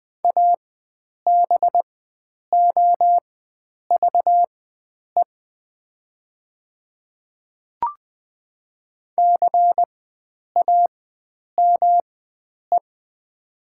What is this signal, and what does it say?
Morse code, heard as a keyed tone, about 20 wpm, spaced as for 9 wpm, 700 Hz. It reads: ABOVE CAME
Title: 4x Character Spacing - Top 500 Words - 20wpm